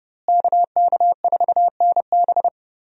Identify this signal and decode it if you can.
Morse code, heard as a keyed tone, about 30 wpm, 700 Hz. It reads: KK4NB
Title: USA Call Signs - 5 Characters - Repeated 3x - 30wpm